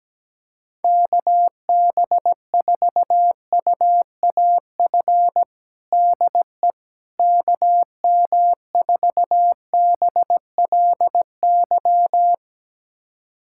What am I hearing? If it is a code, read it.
Morse code, heard as a keyed tone, about 17 wpm, 700 Hz. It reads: KB4UAF DE KM4BLY
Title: USA Callsign DE Callsign 17wpm